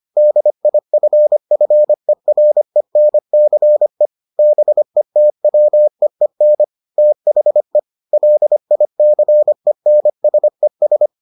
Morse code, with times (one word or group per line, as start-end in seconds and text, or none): 0.2-4.1 DIFFERENCE
4.4-6.6 BETWEEN
7.0-7.8 THE
8.1-11.1 LICENSES